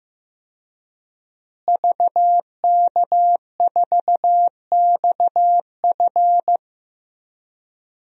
Morse code, heard as a keyed tone, about 15 wpm, 700 Hz. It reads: VK4XF
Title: DXCC-100 Call Signs - 5 Characters - Repeated 2x - 15wpm